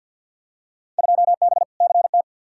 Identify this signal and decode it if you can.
Morse code, heard as a keyed tone, about 46 wpm, 700 Hz. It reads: 2B CT
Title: Field Day Exchanges Encore - 50wpm